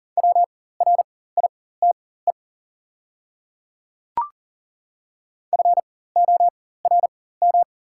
Morse code, heard as a keyed tone, about 40 wpm, spaced as for 19 wpm, 700 Hz. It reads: WRITE FORM
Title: 4x Character Spacing - Top 200 Words - 40wpm